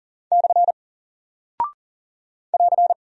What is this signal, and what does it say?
Morse code, heard as a keyed tone, about 40 wpm, 700 Hz. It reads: / <AR>
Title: Single Character (Letter:Number:Punctuation:Prosign) 40wpm